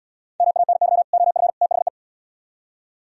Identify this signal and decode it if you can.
Morse code, heard as a keyed tone, about 77 wpm, 700 Hz. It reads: MANY OF THE